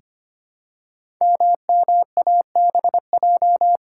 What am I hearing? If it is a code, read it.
Morse code, heard as a keyed tone, about 25 wpm, 690 Hz. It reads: MMABJ
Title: Code Groups - 5 Letter:Numbers - 25wpm